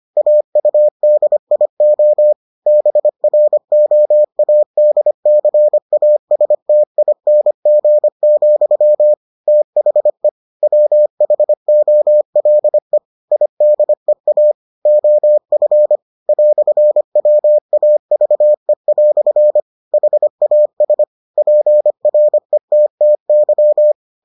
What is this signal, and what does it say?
Morse code, heard as a keyed tone, about 25 wpm, 600 Hz. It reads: AUDIO BROADCASTING, THE WHOLE IDEA OF "WAVE" HAS PRETTY